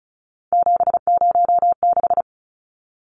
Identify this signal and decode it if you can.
Morse code, heard as a keyed tone, about 34 wpm, 700 Hz. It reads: 706
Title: Numbers - 3-Digit 35wpm